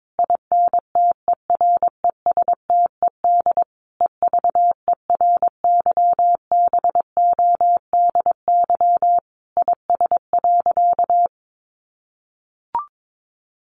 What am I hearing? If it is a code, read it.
Morse code, heard as a keyed tone, about 22 wpm, 700 Hz. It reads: INTERESTED EVERYBODY IS.
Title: Sentences from Top 1400 Words 22wpm